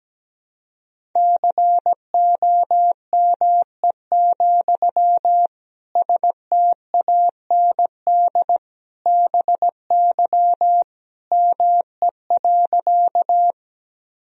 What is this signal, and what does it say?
Morse code, heard as a keyed tone, about 17 wpm, 700 Hz. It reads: COME, STAND BY ME.
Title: Sentences from Top 1000 Words - Review Part 1 - 17wpm